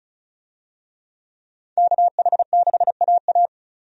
Morse code, heard as a keyed tone, about 36 wpm, 700 Hz. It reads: KH6AA
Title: DXCC-100 Call Signs - 5 Characters - 35wpm